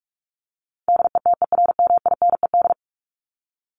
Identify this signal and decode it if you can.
Morse code, heard as a keyed tone, about 44 wpm, 700 Hz. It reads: DETERMINED
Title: ED Suffix - Top 100 Words with Base Words 45wpm